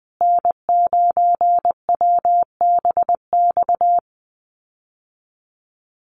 Morse code, heard as a keyed tone, about 19 wpm, 700 Hz. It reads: N9WBX
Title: USA Call Signs Repeated 2x 20wpm